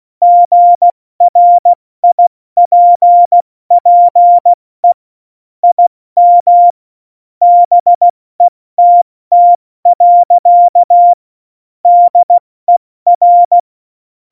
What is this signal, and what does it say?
Morse code, heard as a keyed tone, about 16 wpm, 700 Hz. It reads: GRIPPE IM BETT. DER
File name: Morsekurs / Witze / Witz 30. Januar 2018, 70 F80